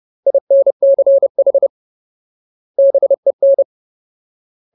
Morse code, heard as a keyed tone, about 30 wpm, 550 Hz. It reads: INCH BEN
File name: Ultimate Word List / 30WPM Ultimate